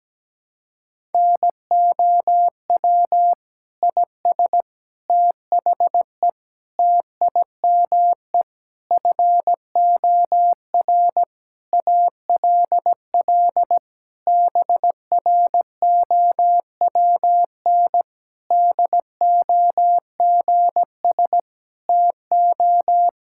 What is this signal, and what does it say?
Morse code, heard as a keyed tone, about 17 wpm, 700 Hz. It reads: NOW IS THE TIME FOR ALL BROWN DOGS TO
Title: Pangram Challenge v2 - 17wpm